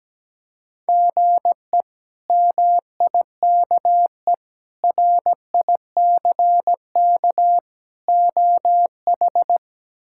Morse code, heard as a keyed tone, about 17 wpm, 700 Hz. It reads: GE MIKE RICK OH
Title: K1USN SST QSO Practice - 17wpm